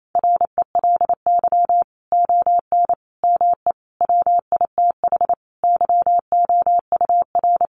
Morse code, heard as a keyed tone, about 28 wpm, 700 Hz. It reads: RELY ON ME WITH YOUR
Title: Sentences from Top 1200 Words 28wpm